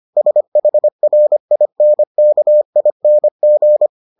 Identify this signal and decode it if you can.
Morse code, heard as a keyed tone, about 25 wpm, 600 Hz. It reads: SHRINKING